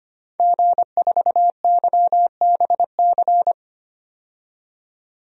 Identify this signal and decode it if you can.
Morse code, heard as a keyed tone, about 25 wpm, 700 Hz. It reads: G4YBC
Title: DXCC-100 Call Signs - 5 Characters - 25wpm